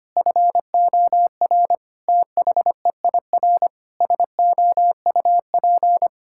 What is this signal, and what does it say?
Morse code, heard as a keyed tone, about 25 wpm, 700 Hz. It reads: FOR THEIR SOUP
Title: Sentences from Top 1000 Words - Review Part 6 - 25wpm